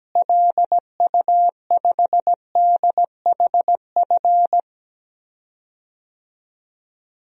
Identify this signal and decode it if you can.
Morse code, heard as a keyed tone, about 17 wpm, 700 Hz. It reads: LU5DHF
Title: DXCC-100 Call Signs - 6 Characters - Repeated 3x - 17wpm